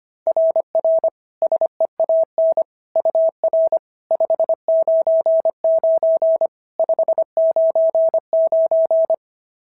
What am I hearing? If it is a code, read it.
Morse code, heard as a keyed tone, about 25 wpm, 650 Hz. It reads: RR SEAN UR 599 599
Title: SOTA QSO Practice 25wpm